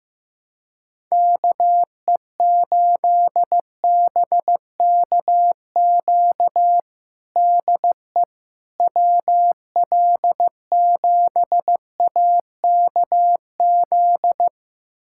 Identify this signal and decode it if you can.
Morse code, heard as a keyed tone, about 15 wpm, 700 Hz. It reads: KE8BKQ DE WL7AKZ
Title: USA Callsign DE Callsign - 6 Characters - 15wpm